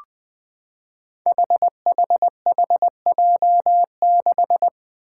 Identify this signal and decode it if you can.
Morse code, heard as a keyed tone, about 20 wpm, 700 Hz. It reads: HHHJ6
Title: Five Challenging Letter-Number Combinations 20wpm